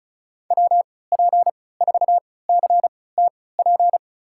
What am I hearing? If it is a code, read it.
Morse code, heard as a keyed tone, about 36 wpm, spaced as for 20 wpm, 700 Hz. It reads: WP4CTP
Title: USA Call Signs - 6 Characters - 3x Character Spacing - 35wpm